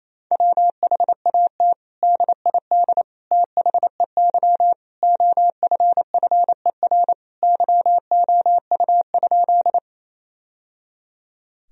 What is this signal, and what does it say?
Morse code, heard as a keyed tone, about 28 wpm, 700 Hz. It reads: WHAT DID THEY OFFER YOU?
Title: Sentences from Top 1000 Words - Review Part 4 - 28wpm